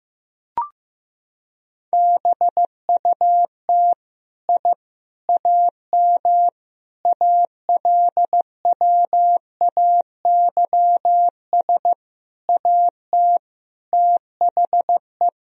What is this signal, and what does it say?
Morse code, heard as a keyed tone, about 15 wpm, 700 Hz. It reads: BUT I AM ALWAYS AT THE
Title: Sentences from Top 1000 Words - Review Part 2 - 15wpm